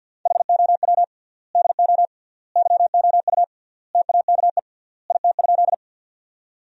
Morse code, heard as a keyed tone, about 50 wpm, 700 Hz. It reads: HOW DO YOU TAKE IT?
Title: 3x Word-Spacing - Sentences from Top 200 Words - 50wpm